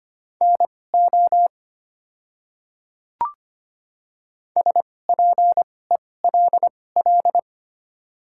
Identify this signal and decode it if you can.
Morse code, heard as a keyed tone, about 25 wpm, spaced as for 18 wpm, 700 Hz. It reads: NO SPELL
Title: 2x Character Spacing - Top 500 Words - 25wpm